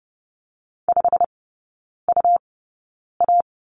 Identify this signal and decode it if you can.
Morse code, heard as a keyed tone, about 30 wpm, spaced as for 9 wpm, 700 Hz. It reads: 5UA
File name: Lesson 17e - Letter u - Three Characters 30wpm